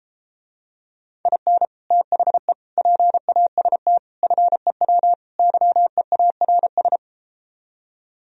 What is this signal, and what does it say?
Morse code, heard as a keyed tone, about 33 wpm, 700 Hz. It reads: IN THE PAST FEW YEARS